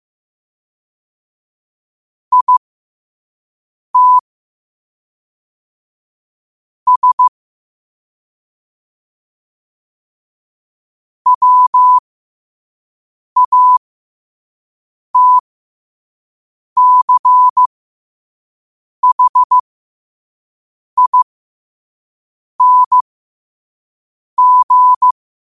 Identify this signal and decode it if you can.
Morse code, heard as a keyed tone, about 15 wpm, spaced as for 5 wpm, 1000 Hz. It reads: IT S WATCHING
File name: qotdfiles / 5wpm